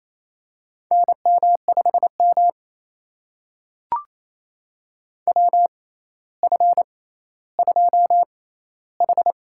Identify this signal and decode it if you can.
Morse code, heard as a keyed tone, about 28 wpm, spaced as for 14 wpm, 700 Hz. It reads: NM5M WF2H
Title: USA Call Signs - 4 Characters - 6x Character Spacing - 28wpm